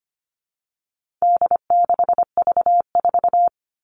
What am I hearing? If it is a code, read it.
Morse code, heard as a keyed tone, about 25 wpm, 700 Hz. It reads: D6V4
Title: All Cut Numbers - 4-Digit 25wpm